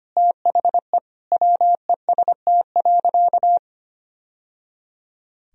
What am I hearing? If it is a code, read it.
Morse code, heard as a keyed tone, about 25 wpm, 700 Hz. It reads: THE WEST.
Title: Sentences from Top 1000 Words - Review Part 2 - 25wpm